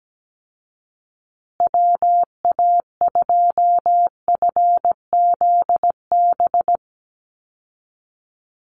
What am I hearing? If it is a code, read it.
Morse code, heard as a keyed tone, about 17 wpm, 700 Hz. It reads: WA2FZB
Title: Call Signs Repeated 2x - 17wpm